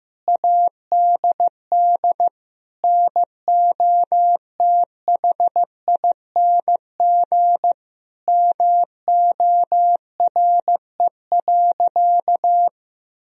Morse code, heard as a keyed tone, about 15 wpm, 700 Hz. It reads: ADD NOTHING MORE.
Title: Sentences from Top 1000 Words - Review Part 1 - 15wpm